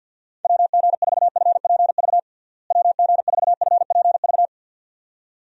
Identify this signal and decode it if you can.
Morse code, heard as a keyed tone, about 52 wpm, 700 Hz. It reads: WG4FPV WG4FPV
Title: USA Call Signs - 6 Characters - Repeated 2x - 50wpm